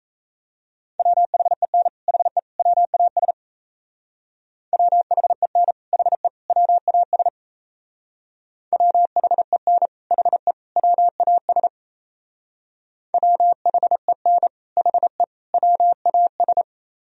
Morse code, heard as a keyed tone, about 36 wpm, 700 Hz. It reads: WHEN HE WAS WHEN HE WAS WHEN HE WAS WHEN HE WAS